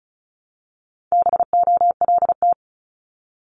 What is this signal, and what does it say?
Morse code, heard as a keyed tone, about 36 wpm, 700 Hz. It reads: BOLT
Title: Multi-letter Phonemes 35wpm